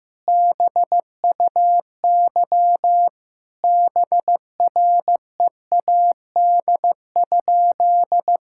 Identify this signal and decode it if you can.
Morse code, heard as a keyed tone, about 15 wpm, 700 Hz. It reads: BUY BREAD?
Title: Sentences from Top 1600 Words 15wpm